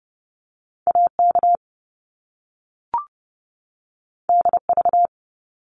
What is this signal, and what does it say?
Morse code, heard as a keyed tone, about 30 wpm, 700 Hz. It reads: A K D V
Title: Top Two Letter Combinations in Words 30wpm